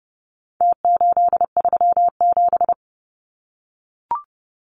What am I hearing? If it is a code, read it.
Morse code, heard as a keyed tone, about 30 wpm, 700 Hz. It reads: T837
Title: Numbers - Contest Style - 4-Digit 30wpm